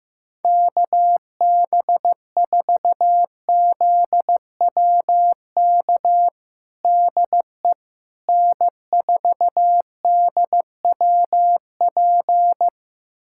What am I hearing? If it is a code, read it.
Morse code, heard as a keyed tone, about 15 wpm, 700 Hz. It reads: KB4ZWK DE N4DWP
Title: USA Callsign DE Callsign 15wpm